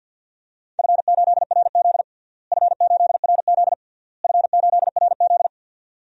Morse code, heard as a keyed tone, about 48 wpm, 700 Hz. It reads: F8RZ F8RZ F8RZ
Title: DXCC-100 Call Signs - 4 Characters - Repeated 3x - 50wpm